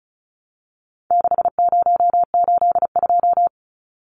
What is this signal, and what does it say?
Morse code, heard as a keyed tone, about 34 wpm, 700 Hz. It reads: 6082
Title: Numbers - 4-Digit 35wpm